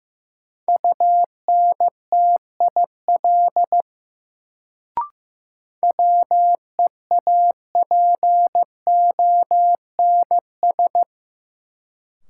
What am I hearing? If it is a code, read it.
Morse code, heard as a keyed tone, about 15 wpm, 700 Hz. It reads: UNTIL WEAPONS
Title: Sets of 5 Words Encore - New Word Drill - 15wpm